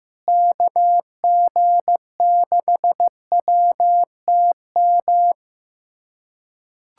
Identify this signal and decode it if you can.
Morse code, heard as a keyed tone, about 15 wpm, 700 Hz. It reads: KG6WTM
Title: USA Callsign DE Callsign 15wpm